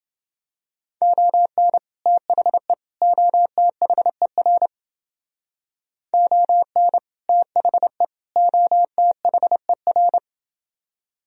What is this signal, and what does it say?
Morse code, heard as a keyed tone, about 30 wpm, 700 Hz. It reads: ON THE OTHER ON THE OTHER